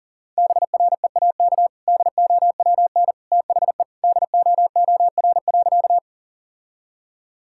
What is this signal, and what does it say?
Morse code, heard as a keyed tone, about 37 wpm, 700 Hz. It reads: BREAK DOWN THE DOOR.
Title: Sentences from Top 1000 Words - Review Part 6 - 40wpm